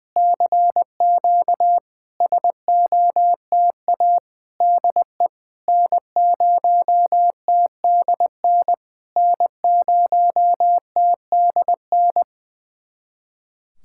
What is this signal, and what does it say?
Morse code, heard as a keyed tone, about 20 wpm, 700 Hz. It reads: CQ SOTA DE N0TDN N0TDN
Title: SOTA QSO Practice 20wpm